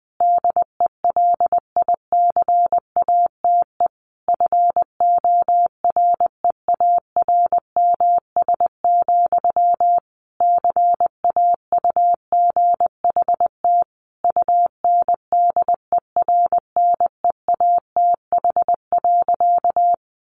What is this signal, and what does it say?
Morse code, heard as a keyed tone, about 20 wpm, 700 Hz. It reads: DELICATE FOREARMS, CAUGHT UNDERNEATH.
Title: A Sound Of Thunder 20wpm